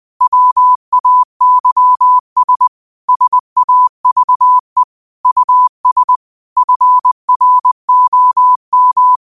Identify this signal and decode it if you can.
Morse code, heard as a keyed tone, about 20 wpm, 1000 Hz. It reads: WAYS SAVE US FROM
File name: qotdfiles / 20wpm